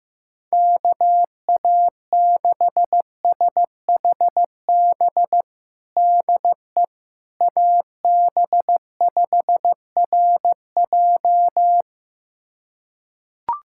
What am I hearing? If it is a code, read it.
Morse code, heard as a keyed tone, about 15 wpm, 700 Hz. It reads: KA6SHB DE AB5RJ
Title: USA Callsign DE Callsign 15wpm